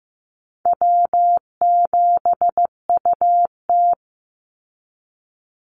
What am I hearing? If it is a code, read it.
Morse code, heard as a keyed tone, about 15 wpm, 700 Hz. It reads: W7UT
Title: Easy Call Signs Repeated 2x - 15wpm